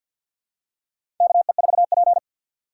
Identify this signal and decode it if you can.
Morse code, heard as a keyed tone, about 51 wpm, 700 Hz. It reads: KE4P
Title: USA Call Signs - 4 Characters - 2x Character Spacing - 50wpm